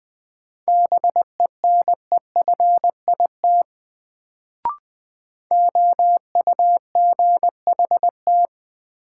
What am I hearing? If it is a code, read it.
Morse code, heard as a keyed tone, about 20 wpm, 700 Hz. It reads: BENEFIT OUGHT
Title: Sets of 5 Words Encore - New Word Drill - 20wpm